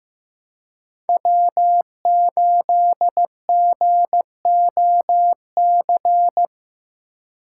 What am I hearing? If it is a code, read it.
Morse code, heard as a keyed tone, about 15 wpm, 700 Hz. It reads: W8GOC
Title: USA Call Signs - 5 Characters - 4x Character Spacing - 15wpm